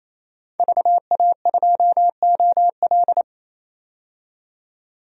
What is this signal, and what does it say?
Morse code, heard as a keyed tone, about 28 wpm, 700 Hz. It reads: VA2OL
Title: DXCC-100 Call Signs - 5 Characters - 28wpm